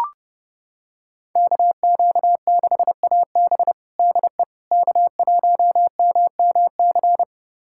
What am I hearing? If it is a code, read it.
Morse code, heard as a keyed tone, about 32 wpm, 700 Hz. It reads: KQ6AB DE K1MMC
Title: USA Callsign DE Callsign - 5 Characters - 30wpm